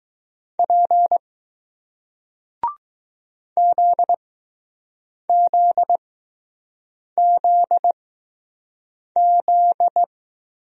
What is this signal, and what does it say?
Morse code, heard as a keyed tone, about 20 wpm, spaced as for 6 wpm, 700 Hz. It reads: P ZZZZ